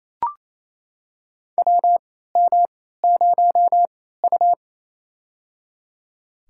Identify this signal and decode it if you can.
Morse code, heard as a keyed tone, about 28 wpm, spaced as for 16 wpm, 700 Hz. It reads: WM0U
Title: USA Call Signs - 4 Characters - 3x Character Spacing - 28wpm